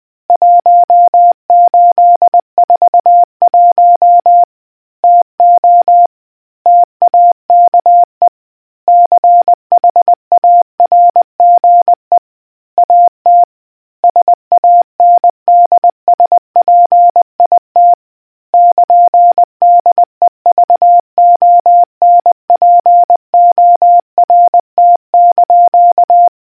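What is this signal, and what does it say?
Morse code, heard as a keyed tone, about 20 wpm, 700 Hz. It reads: 1841 TO TAKE CHARGE AT SANDSPIT <KN>DEVONPORT)
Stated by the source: Morse code history quiz question from a New Zealand naval communicators association.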